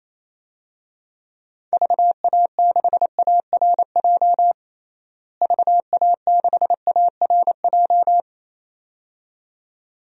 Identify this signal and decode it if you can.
Morse code, heard as a keyed tone, about 28 wpm, 700 Hz. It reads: VA6ARJ VA6ARJ
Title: DXCC-100 Call Signs - 6 Characters - Repeated 2x - 28wpm